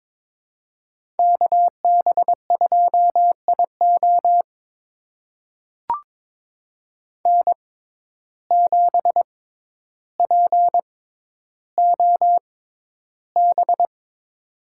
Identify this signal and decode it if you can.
Morse code, heard as a keyed tone, about 21 wpm, spaced as for 11 wpm, 700 Hz. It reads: KB2IO N7POB
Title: USA Call Signs - 5 Characters - 6x Character Spacing - 22wpm